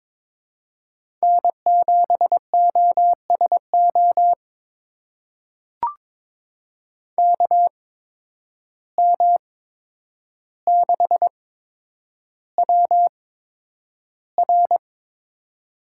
Morse code, heard as a keyed tone, about 22 wpm, spaced as for 9 wpm, 700 Hz. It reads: N7OSO KM6WR
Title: USA Call Signs - 5 Characters - 8x Character Spacing - 22wpm